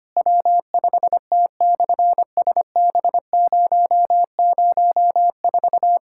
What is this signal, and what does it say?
Morse code, heard as a keyed tone, about 25 wpm, 700 Hz. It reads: W5T/SB004
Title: SOTA QSO Practice 25wpm